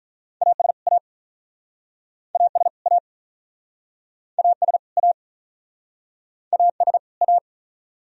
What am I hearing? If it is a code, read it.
Morse code, heard as a keyed tone, about 46 wpm, 700 Hz. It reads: AS A AS A AS A AS A